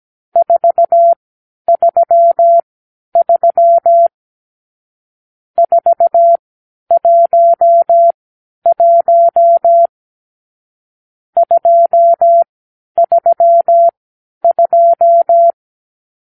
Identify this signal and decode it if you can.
Morse code, heard as a keyed tone, about 17 wpm, spaced as for 10 wpm, 680 Hz. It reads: 433 411 232
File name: k7qo / Lesson 46 (1-4)